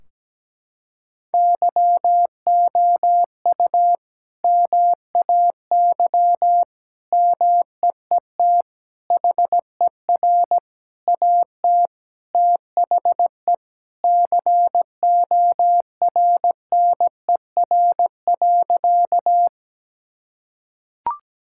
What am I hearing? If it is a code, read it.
Morse code, heard as a keyed tone, about 17 wpm, 700 Hz. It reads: YOU MAY MEET HER AT THE CORNER.
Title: Sentences from Top 1600 Words 17wpm